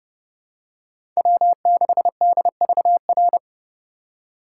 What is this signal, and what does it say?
Morse code, heard as a keyed tone, about 30 wpm, 700 Hz. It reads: W6DVR
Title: USA Call Signs - 5 Characters - 4x Character Spacing - 30wpm